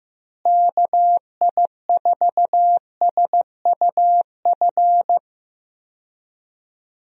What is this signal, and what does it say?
Morse code, heard as a keyed tone, about 15 wpm, 700 Hz. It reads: KI4SUF
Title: USA Call Signs - 6 Characters - Repeated 2x - 15wpm